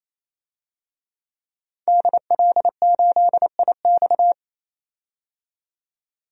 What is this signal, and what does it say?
Morse code, heard as a keyed tone, about 27 wpm, 700 Hz. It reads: DL8IX
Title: DXCC-100 Call Signs - 5 Characters - 28wpm